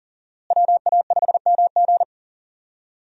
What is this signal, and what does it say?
Morse code, heard as a keyed tone, about 40 wpm, 700 Hz. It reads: WA5MG
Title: USA Call Signs - 5 Characters - Repeated 2x - 40wpm